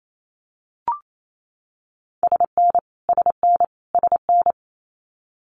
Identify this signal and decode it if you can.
Morse code, heard as a keyed tone, about 28 wpm, 700 Hz. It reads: SN SN SN
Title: Abbreviations Repeated 3x - 28wpm